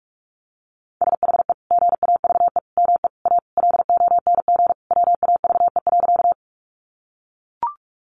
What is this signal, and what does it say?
Morse code, heard as a keyed tone, about 44 wpm, 700 Hz. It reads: SHE GAVE ME A LONG WAVE.
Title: Sentences from Top 1400 Words 45wpm